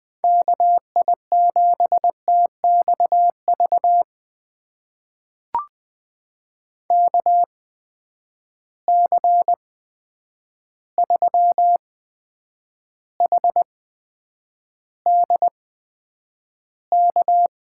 Morse code, heard as a keyed tone, about 20 wpm, spaced as for 9 wpm, 700 Hz. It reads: KI7TXV KC3HDK
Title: USA Call Signs - 6 Characters - 8x Character Spacing - 20wpm